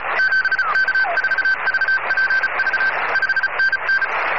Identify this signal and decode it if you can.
Morse code, heard as a keyed tone, about 34 wpm, 1550 Hz. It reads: 7K4VPV 5NN